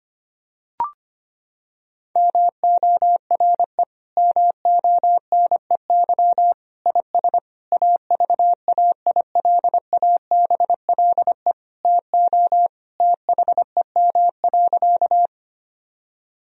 Morse code, heard as a keyed tone, about 25 wpm, 700 Hz. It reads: MORE MONEY IS AVAILABLE TO THEM.
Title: Sentences from Top 1000 Words - Review Part 3 - 25wpm